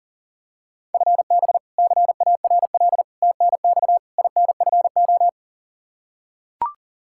Morse code, heard as a keyed tone, about 40 wpm, 700 Hz. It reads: FB CARL TNX INFO
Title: Practice QSO - Exchange 3 - More Info - 40wpm